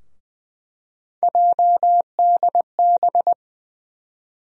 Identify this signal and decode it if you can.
Morse code, heard as a keyed tone, about 20 wpm, 700 Hz. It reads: JDB
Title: USA Call Sign Postfixes - 3 Letters 20wpm